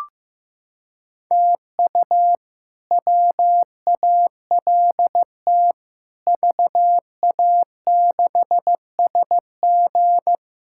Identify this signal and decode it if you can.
Morse code, heard as a keyed tone, about 15 wpm, 700 Hz. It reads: TU WALT VA6SG
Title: K1USN SST QSO Practice - 15wpm